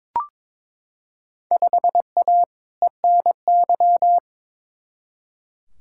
Morse code, heard as a keyed tone, about 21 wpm, 700 Hz. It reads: 5A ENY
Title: Field Day Exchanges Encore - 22wpm